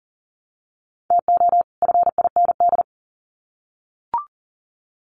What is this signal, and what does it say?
Morse code, heard as a keyed tone, about 40 wpm, 700 Hz. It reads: TO FIND
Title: Sets of 2 Words 40wpm